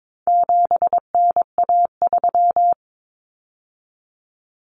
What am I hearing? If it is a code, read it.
Morse code, heard as a keyed tone, about 22 wpm, 700 Hz. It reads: 7NA3
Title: Common Cut Numbers - 4-Digit 22wpm